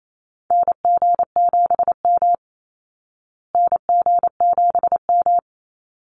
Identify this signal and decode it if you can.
Morse code, heard as a keyed tone, about 28 wpm, 700 Hz. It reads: NG7M NG7M
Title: Call Signs Repeated 3x - 28wpm